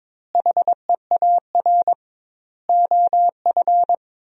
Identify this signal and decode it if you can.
Morse code, heard as a keyed tone, about 22 wpm, 700 Hz. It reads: HEAR OF
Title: 2x Word-Spacing - Sentences from Top 400 Words - 22wpm